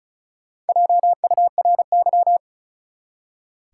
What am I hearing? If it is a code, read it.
Morse code, heard as a keyed tone, about 36 wpm, 700 Hz. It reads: JURY
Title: Four Letter Words - 35wpm